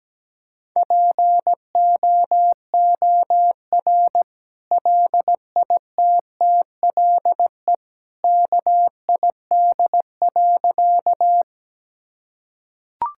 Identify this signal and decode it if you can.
Morse code, heard as a keyed tone, about 17 wpm, 700 Hz. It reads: POOR LITTLE KID.
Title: Sentences from Top 1600 Words 17wpm